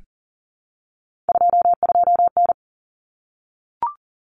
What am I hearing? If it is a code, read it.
Morse code, heard as a keyed tone, about 40 wpm, 700 Hz. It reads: 22N
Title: Common Cut Numbers - 3-Digit 40wpm